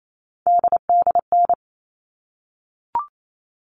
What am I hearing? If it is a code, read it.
Morse code, heard as a keyed tone, about 28 wpm, 700 Hz. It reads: DDN
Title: All Cut Numbers - 3-Digit 28wpm